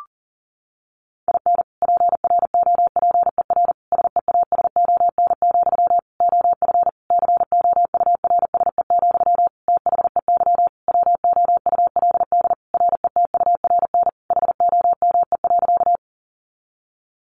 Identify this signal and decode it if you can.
Morse code, heard as a keyed tone, about 37 wpm, 700 Hz. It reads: IN PROPER SEASON, OF COURSE, THEY WOULD RETURN HOME.
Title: Sentences from Top 1400 Words 40wpm